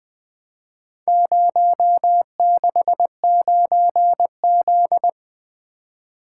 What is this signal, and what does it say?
Morse code, heard as a keyed tone, about 23 wpm, 700 Hz. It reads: 069Z
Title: Speed-Racing - Four Letter-Number Combinations - 20wpm